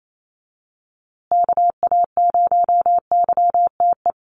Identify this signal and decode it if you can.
Morse code, heard as a keyed tone, about 28 wpm, 700 Hz. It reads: KA0YTE
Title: USA Call Signs Repeated 2x 28wpm